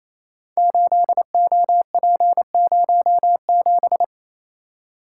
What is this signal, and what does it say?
Morse code, heard as a keyed tone, about 28 wpm, 700 Hz. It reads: 8OP07
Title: Code Groups - 5 Letter:Numbers - 28wpm